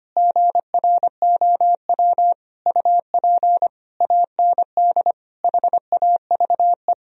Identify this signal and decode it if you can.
Morse code, heard as a keyed tone, about 25 wpm, 700 Hz. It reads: GROW UP AND HAVE
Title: Sentences from Top 1000 Words - Review Part 1 - 25wpm